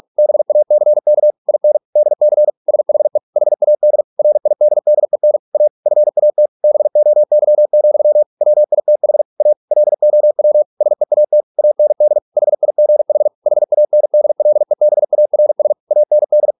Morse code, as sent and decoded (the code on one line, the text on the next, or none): -... .- -.-. -.- / .. -. / -.. -.-. / ... .... / .... .- -.. / .-. .. -.. -.. -. / .- / ..-. .- - / -... --- -.-- --..-- / .-- .. - .... / .- / .-.. --- .-- / ... .- - / .- -. -.. / .... .. --. .... / .... .- -. -.. .-.. -... .- .-. ... / .- -. -..
BACK IN DC SHE HAD RIDDEN A FAT BOY, WITH A LOW SEAT AND HIGH HANDLEBARS AND